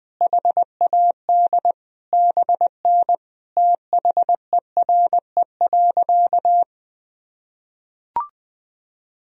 Morse code, heard as a keyed tone, about 20 wpm, 700 Hz. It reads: HAD BN THERE.
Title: Sentences from Top 300 Words and Abbreviations 20wpm